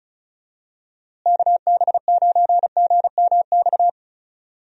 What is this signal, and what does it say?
Morse code, heard as a keyed tone, about 36 wpm, 700 Hz. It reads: KB9GMX
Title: USA Call Signs - 6 Characters - Repeated 3x - 35wpm